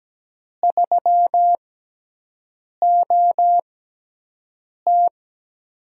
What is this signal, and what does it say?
Morse code, heard as a keyed tone, about 17 wpm, spaced as for 6 wpm, 700 Hz. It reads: 3OT
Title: USA Call Signs - 4 Characters - 6x Character Spacing - 17wpm